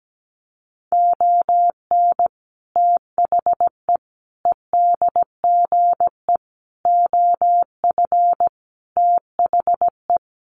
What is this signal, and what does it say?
Morse code, heard as a keyed tone, about 17 wpm, 700 Hz. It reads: ON THE EDGE OF THE
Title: Sets of 5 Words 17wpm